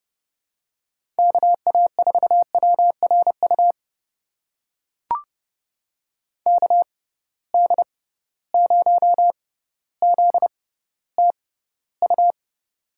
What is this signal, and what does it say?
Morse code, heard as a keyed tone, about 30 wpm, spaced as for 15 wpm, 700 Hz. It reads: KA4WRU KD0ZTU
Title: USA Call Signs - 6 Characters - 6x Character Spacing - 30wpm